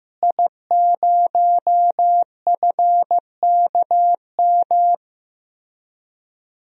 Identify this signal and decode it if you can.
Morse code, heard as a keyed tone, about 16 wpm, 700 Hz. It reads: I0FKM
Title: DXCC-100 Call Signs - 5 Characters - Repeated 3x - 15wpm